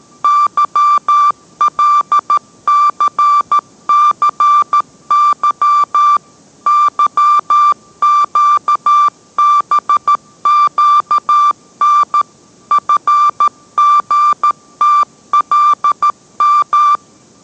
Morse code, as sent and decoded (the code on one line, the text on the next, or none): -.-- .-.. -.-. -.-. -.-- / -.-- --.- -... --.- -. / ..-. --. - .-.. --
YLCCY YQBQN FGTLM